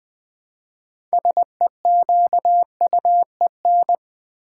Morse code, heard as a keyed tone, about 20 wpm, 700 Hz. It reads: SEQUEN
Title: Build A Word - 2 Letters At A Time - 8-Letter Words - 20wpm